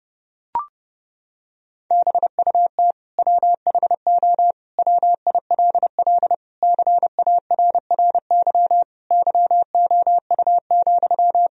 Morse code, heard as a keyed tone, about 32 wpm, 700 Hz. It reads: BUT WHO WILL CARRY YOU,
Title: Sentences from Top 1000 Words - Review Part 1 - 30wpm